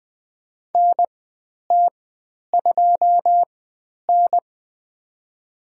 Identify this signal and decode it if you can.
Morse code, heard as a keyed tone, about 20 wpm, spaced as for 10 wpm, 700 Hz. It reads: NT2N
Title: 095 Lesson 16g - Letter c - Call Signs 20wpm